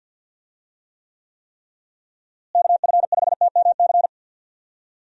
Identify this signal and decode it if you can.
Morse code, heard as a keyed tone, about 51 wpm, 700 Hz. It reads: KF5TMC
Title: USA Call Signs - 6 Characters - Repeated 3x - 50wpm